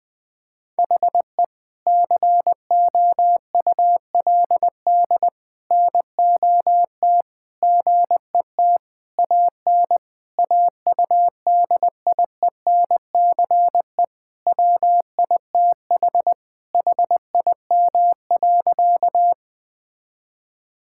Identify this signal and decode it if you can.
Morse code, heard as a keyed tone, about 21 wpm, 700 Hz. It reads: HE COULD NOT GET AN AUDIENCE WITH HIM.
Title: Sentences from Top 1000 Words - Review Part 5 - 20wpm